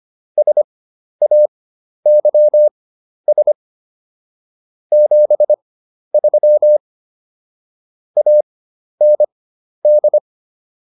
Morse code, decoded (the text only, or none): SAYS 73 AND